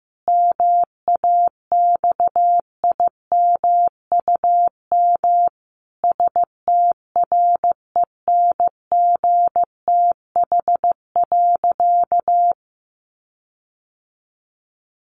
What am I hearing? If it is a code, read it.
Morse code, heard as a keyed tone, about 14 wpm, 700 Hz. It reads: MAXIMUM STRENGTH.
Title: Sentences from Top 1300 Words 15wpm